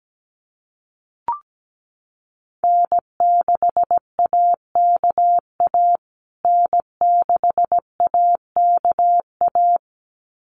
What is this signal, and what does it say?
Morse code, heard as a keyed tone, about 17 wpm, 700 Hz. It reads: N6AKA N6AKA
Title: USA Call Signs Repeated 2x 17wpm